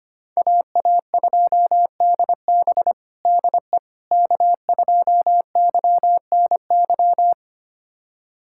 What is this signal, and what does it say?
Morse code, heard as a keyed tone, about 25 wpm, 700 Hz. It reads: AA2DB DE K2YNY
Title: USA Callsign DE Callsign - 5 Characters - 25wpm